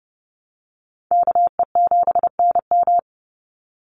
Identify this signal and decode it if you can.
Morse code, heard as a keyed tone, about 30 wpm, 700 Hz. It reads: KE7NM
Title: USA Call Signs 30wpm